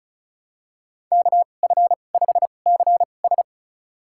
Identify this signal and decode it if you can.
Morse code, heard as a keyed tone, about 36 wpm, spaced as for 25 wpm, 700 Hz. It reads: KF5CS
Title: USA Call Signs - 5 Characters - 2x Character Spacing - 35wpm